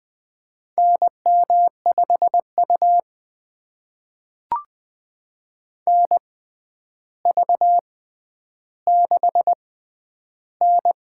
Morse code, heard as a keyed tone, about 20 wpm, spaced as for 10 wpm, 700 Hz. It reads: NM5U NV6N
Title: USA Call Signs - 4 Characters - 6x Character Spacing - 20wpm